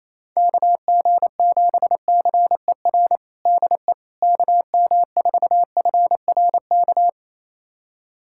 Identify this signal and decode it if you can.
Morse code, heard as a keyed tone, about 28 wpm, 700 Hz. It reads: KG7CER DE KM4FRK
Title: USA Callsign DE Callsign 28wpm